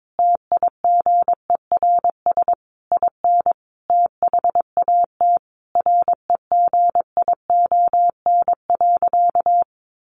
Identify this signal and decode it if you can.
Morse code, heard as a keyed tone, about 22 wpm, 700 Hz. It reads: TIGERS IN THAT REGION.
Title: A Sound Of Thunder 22wpm